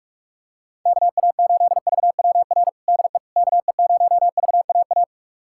Morse code, heard as a keyed tone, about 46 wpm, 700 Hz. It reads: KA8VWR DE KE0VAA
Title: USA Callsign DE Callsign - 6 Characters - 45wpm